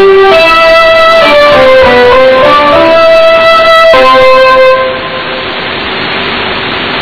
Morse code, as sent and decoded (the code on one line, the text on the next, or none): none